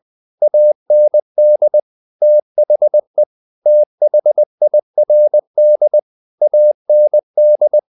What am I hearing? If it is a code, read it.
Morse code, heard as a keyed tone, about 20 wpm, 600 Hz. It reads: AND THE THIRD AND
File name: the old man and the sea / the old man and the sea part 1 20wpm